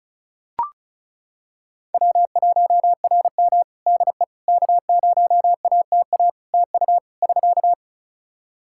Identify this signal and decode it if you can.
Morse code, heard as a keyed tone, about 36 wpm, 700 Hz. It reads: W1RM DE K0ATA TU <SK>
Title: Practice QSO - Exchange 4 - Ending - 35wpm